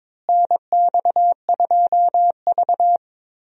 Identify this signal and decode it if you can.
Morse code, heard as a keyed tone, about 22 wpm, 700 Hz. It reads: NX2V
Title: USA Call Signs - 4 Characters - Repeated 3x - 22wpm